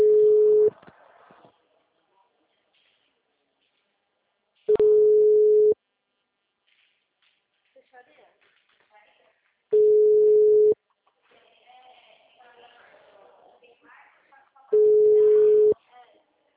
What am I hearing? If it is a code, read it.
Ringback tone (Europe, 425 Hz), with speech.
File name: out- -2006- - - .39198